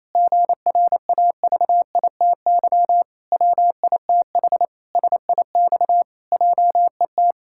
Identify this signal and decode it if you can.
Morse code, heard as a keyed tone, about 28 wpm, 700 Hz. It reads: GRAVITY WITH SIX JET
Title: Pangram Challenge v2 - 28wpm